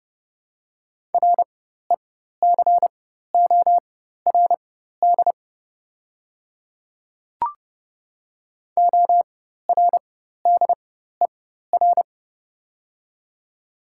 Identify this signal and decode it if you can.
Morse code, heard as a keyed tone, about 32 wpm, spaced as for 14 wpm, 690 Hz. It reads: RECORD ORDER
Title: 4x Character Spacing - Top 500 Words - 30wpm